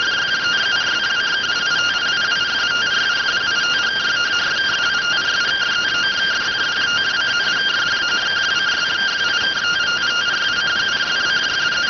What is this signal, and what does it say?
Two-tone teleprinter signal, about 170 Hz shift.